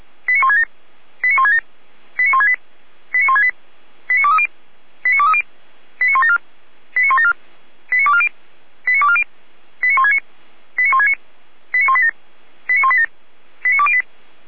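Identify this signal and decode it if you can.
Five-tone selective call (ZVEI), heard as quick runs of five tones, tones of about 70 ms. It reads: 89167 89167 89168 89168 89230 89230 89174 89174 89240 89240 89169 89169 89187 89187 89208
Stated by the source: Five-tone ZVEI selective call 89097, received on an SDR.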